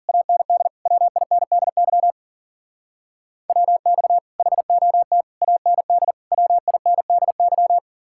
Morse code, heard as a keyed tone, about 52 wpm, 700 Hz. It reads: AND WINDY WX HOT AND WINDY